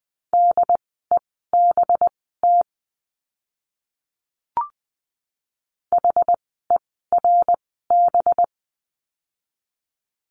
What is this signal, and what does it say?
Morse code, heard as a keyed tone, about 20 wpm, spaced as for 14 wpm, 700 Hz. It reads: DEBT HERB
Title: 185 Lesson 30f - Letter b - Words 20wpm